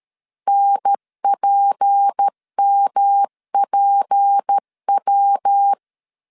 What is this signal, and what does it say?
Morse code, heard as a keyed tone, about 13 wpm, 800 Hz. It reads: NPMPW
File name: LEZIONI / 6 PKX0 / 6 P 12wpm 2033